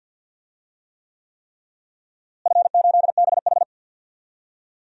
Morse code, heard as a keyed tone, about 48 wpm, 700 Hz. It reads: F8BH
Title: DXCC-100 Call Signs - 4 Characters - Repeated 3x - 50wpm